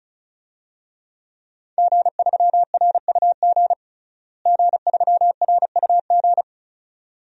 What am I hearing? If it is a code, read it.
Morse code, heard as a keyed tone, about 36 wpm, 700 Hz. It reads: G3RUG G3RUG
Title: DXCC-100 Call Signs - 5 Characters - Repeated 2x - 35wpm